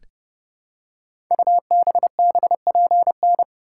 Morse code, heard as a keyed tone, about 30 wpm, 700 Hz. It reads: UBBPN
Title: Code Groups - 5 Letter:Numbers - 30wpm